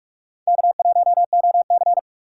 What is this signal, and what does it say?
Morse code, heard as a keyed tone, about 44 wpm, 700 Hz. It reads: K1OC